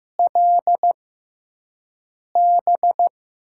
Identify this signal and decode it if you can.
Morse code, heard as a keyed tone, about 15 wpm, spaced as for 5 wpm, 700 Hz. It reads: LB
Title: USA Call Signs - 6 Characters - 6x Character Spacing - 15wpm